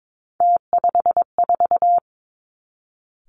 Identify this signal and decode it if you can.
Morse code, heard as a keyed tone, about 22 wpm, 700 Hz. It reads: T54
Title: Three Challenging Letter-Number Combinations 22wpm